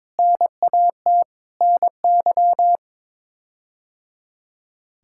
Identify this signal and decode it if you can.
Morse code, heard as a keyed tone, about 22 wpm, 700 Hz. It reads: NAT NY
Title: K1USN SST QSO Practice - 22wpm